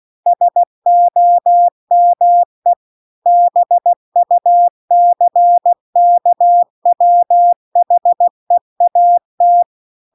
Morse code, heard as a keyed tone, about 16 wpm, 700 Hz. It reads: SOME BUCKWHEAT
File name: CW-The-Big-Twoheaded-River-2-16WPM